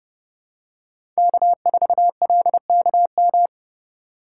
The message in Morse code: K4LKM

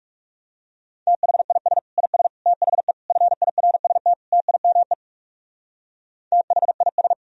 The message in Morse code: THIS IS THE FIRST TIME THIS

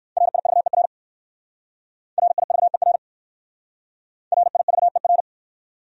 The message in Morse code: RIVER RIVER RIVER

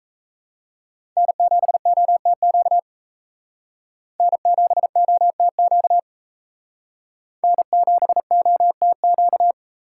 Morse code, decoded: N7OTQ N7OTQ N7OTQ

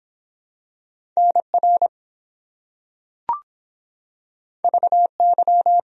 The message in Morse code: NR VY

